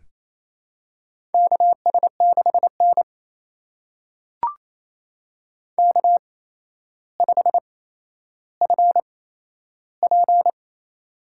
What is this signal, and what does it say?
Morse code, heard as a keyed tone, about 28 wpm, spaced as for 12 wpm, 700 Hz. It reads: KS6N K5FP
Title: USA Call Signs - 4 Characters - 8x Character Spacing - 28wpm